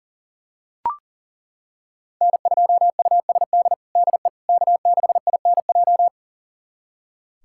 Morse code, heard as a keyed tone, about 40 wpm, 700 Hz. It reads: N2USD DE K6INJ
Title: USA Callsign DE Callsign - 5 Characters - 40wpm